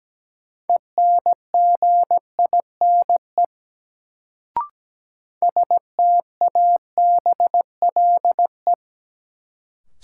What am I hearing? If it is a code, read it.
Morse code, heard as a keyed tone, about 17 wpm, 700 Hz. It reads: ENGINE STABLE
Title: Sentences from Top 1500 Words - New Word Drill Alt - 17wpm